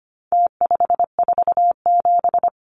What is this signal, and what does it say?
Morse code, heard as a keyed tone, about 25 wpm, 700 Hz. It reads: T547
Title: Numbers - Contest Style - 4-Digit 25wpm